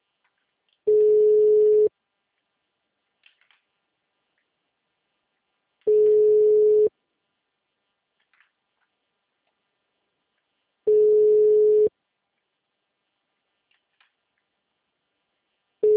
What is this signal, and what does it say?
Ringback tone (Europe, 425 Hz).
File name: out- -2011- - - .14991